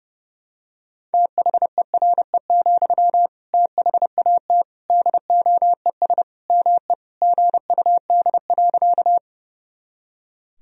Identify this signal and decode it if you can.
Morse code, heard as a keyed tone, about 30 wpm, 700 Hz. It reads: THERE, THAT DOES ME GUD.
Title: Sentences from Top 200 Words and Abbreviations 30wpm